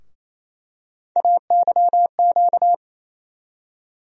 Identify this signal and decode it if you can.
Morse code, heard as a keyed tone, about 27 wpm, 700 Hz. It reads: AYQ
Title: USA Call Sign Postfixes - 3 Letters 28wpm